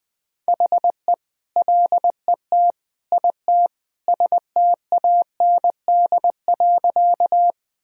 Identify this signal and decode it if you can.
Morse code, heard as a keyed tone, about 21 wpm, 700 Hz. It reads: HE LET IT STAND.
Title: Sentences from Top 1000 Words - Review Part 1 - 20wpm